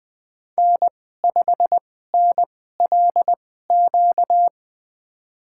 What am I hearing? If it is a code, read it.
Morse code, heard as a keyed tone, about 20 wpm, spaced as for 15 wpm, 700 Hz. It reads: N5NLQ
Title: USA Call Signs - 5 Characters - 2x Character Spacing - 20wpm